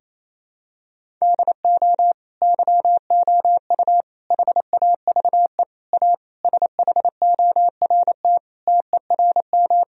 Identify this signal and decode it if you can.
Morse code, heard as a keyed tone, about 28 wpm, 700 Hz. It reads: DO YOU HAVE A SHORT TERM
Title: Sentences from Top 1000 Words - Review Part 4 - 28wpm